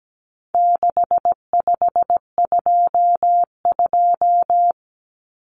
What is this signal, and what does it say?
Morse code, heard as a keyed tone, about 17 wpm, 700 Hz. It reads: 6522
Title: Common Cut Numbers - 4-Digit 17wpm